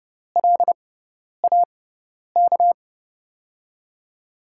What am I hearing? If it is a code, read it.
Morse code, heard as a keyed tone, about 32 wpm, spaced as for 10 wpm, 700 Hz. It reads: LAK
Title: USA Call Signs - 6x Character Spacing - 30wpm